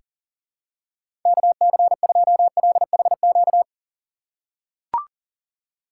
Morse code, heard as a keyed tone, about 37 wpm, 700 Hz. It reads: KC2LHQ
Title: USA Call Signs - 6 Characters - 3x Character Spacing - 40wpm